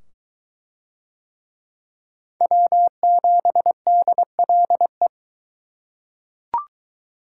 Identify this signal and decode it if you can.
Morse code, heard as a keyed tone, about 20 wpm, 700 Hz. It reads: W7DLE